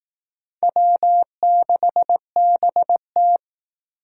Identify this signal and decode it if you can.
Morse code, heard as a keyed tone, about 18 wpm, 700 Hz. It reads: W6BT
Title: Speed Racing - USA 1x2 Call Signs 15wpm